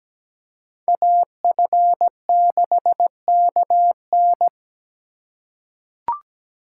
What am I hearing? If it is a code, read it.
Morse code, heard as a keyed tone, about 17 wpm, 700 Hz. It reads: AF6KN
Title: USA Call Signs - 3x Character Spacing - 17wpm